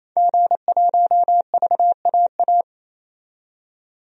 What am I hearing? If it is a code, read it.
Morse code, heard as a keyed tone, about 28 wpm, 700 Hz. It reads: G1VAA
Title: DXCC-100 Call Signs - 5 Characters - 28wpm